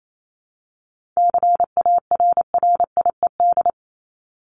Morse code, heard as a keyed tone, about 28 wpm, 700 Hz. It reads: CARRIED
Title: ED Suffix - Top 100 Words with Base Words 28wpm